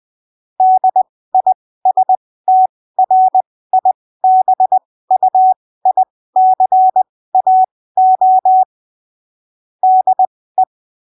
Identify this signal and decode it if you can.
Morse code, heard as a keyed tone, about 20 wpm, spaced as for 15 wpm, 750 Hz. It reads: DISTRIBUICAO DE
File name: arquivos / texto / texto 07 20ppm